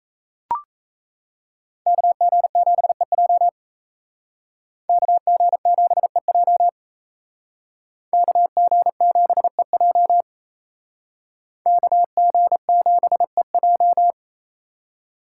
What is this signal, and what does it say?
Morse code, heard as a keyed tone, about 36 wpm, 700 Hz. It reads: KG7EJ KG7EJ KG7EJ KG7EJ